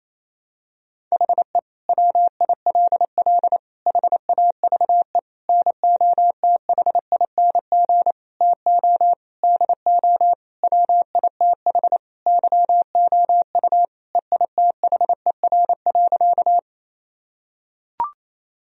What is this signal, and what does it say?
Morse code, heard as a keyed tone, about 28 wpm, 700 Hz. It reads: HE WILL HAVE NOTHING TO DO WITH YOU EITHER.
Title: Sentences from Top 1000 Words - Review Part 4 - 28wpm